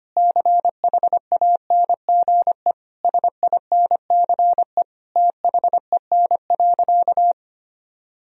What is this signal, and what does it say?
Morse code, heard as a keyed tone, about 25 wpm, 700 Hz. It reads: CHANGE SINCE THEN.
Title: Sentences from Top 1000 Words - Review Part 1 - 25wpm